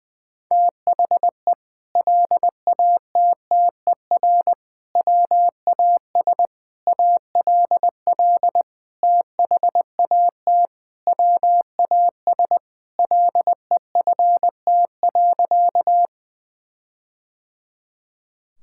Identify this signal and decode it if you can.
Morse code, heard as a keyed tone, about 21 wpm, 700 Hz. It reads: THE LATTER WAS ALL THAT WAS LEFT.
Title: Sentences from Top 1600 Words 20wpm